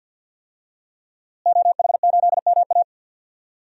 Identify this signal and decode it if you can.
Morse code, heard as a keyed tone, about 50 wpm, 700 Hz. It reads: OH8MA
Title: DXCC-100 Call Signs - 5 Characters - Repeated 3x - 50wpm